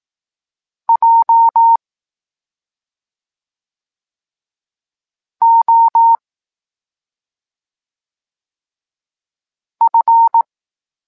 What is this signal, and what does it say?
Morse code, heard as a keyed tone, about 18 wpm, 920 Hz. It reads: J O F